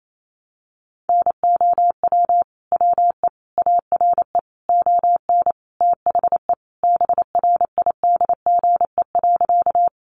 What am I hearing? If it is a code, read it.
Morse code, heard as a keyed tone, about 28 wpm, 700 Hz. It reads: NOW WE ARE ON THE BRIDGE.
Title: Sentences from Top 1200 Words 28wpm